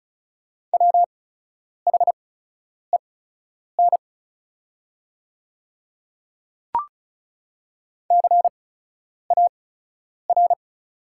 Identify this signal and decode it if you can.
Morse code, heard as a keyed tone, about 36 wpm, spaced as for 10 wpm, 700 Hz. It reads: WHEN CAR
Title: 8x Character Spacing - Top 300 Words - 35wpm